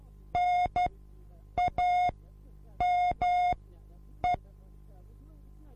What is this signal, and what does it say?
Morse code, heard as a keyed tone, about 12 wpm, spaced as for 8 wpm, 720 Hz. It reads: NAME
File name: repos / Morsekurs / DJ4UF Morsen24